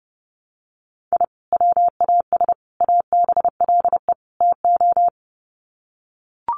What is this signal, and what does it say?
Morse code, heard as a keyed tone, about 30 wpm, 700 Hz. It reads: I WAS ABLE TO
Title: Sets of 4 Words 30wpm